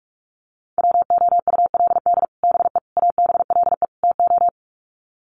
Morse code, heard as a keyed tone, about 44 wpm, 700 Hz. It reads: WOULD BE ABLE TO
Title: Sets of 4 Words 45wpm